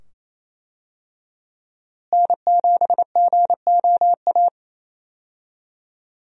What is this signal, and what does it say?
Morse code, heard as a keyed tone, about 28 wpm, 700 Hz. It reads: N7GOA
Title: USA Call Signs - 5 Characters - Repeated 2x - 28wpm